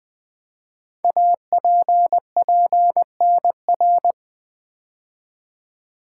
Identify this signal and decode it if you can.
Morse code, heard as a keyed tone, about 21 wpm, 700 Hz. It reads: APPNR
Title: Code Groups - 5 Letter:Numbers - 20wpm